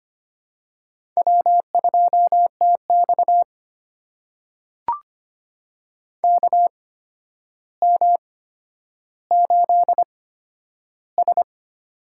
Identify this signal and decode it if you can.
Morse code, heard as a keyed tone, about 25 wpm, spaced as for 11 wpm, 700 Hz. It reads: W2TX KM8S
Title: USA Call Signs - 4 Characters - 8x Character Spacing - 25wpm